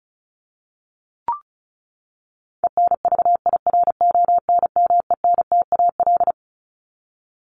Morse code, heard as a keyed tone, about 34 wpm, 700 Hz. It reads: ENVIRONMENTAL
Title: Top 1001-1250 Words 35wpm